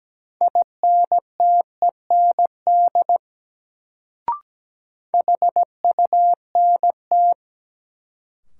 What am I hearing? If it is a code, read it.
Morse code, heard as a keyed tone, about 17 wpm, 700 Hz. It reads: INTEND HUNT
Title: Sentences from Top 1600 Words - New Word Drill - 17wpm